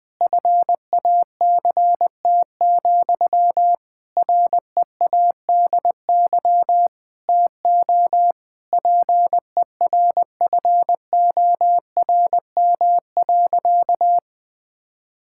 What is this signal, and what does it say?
Morse code, heard as a keyed tone, about 20 wpm, 700 Hz. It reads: FACT, READY TO PERFORM.
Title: Sentences from Top 1000 Words - Review Part 5 - 20wpm